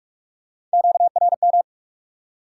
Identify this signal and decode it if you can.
Morse code, heard as a keyed tone, about 44 wpm, 700 Hz. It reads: QRM
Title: Top 180 QSO Elements 45wpm